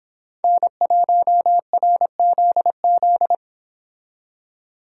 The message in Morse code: N1RZZ

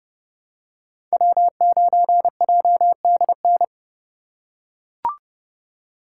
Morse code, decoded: W9JDN

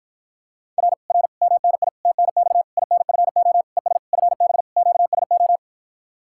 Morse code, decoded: R R MNI TNX INFO ES FB QSO

